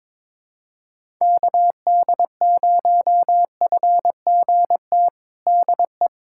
Morse code, heard as a keyed tone, about 21 wpm, 700 Hz. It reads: KD0FGT DE
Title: USA Callsign DE Callsign - 6 Characters - 22wpm